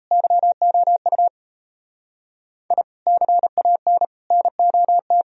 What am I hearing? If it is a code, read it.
Morse code, heard as a keyed tone, about 36 wpm, 700 Hz. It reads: YOU I CAN NOT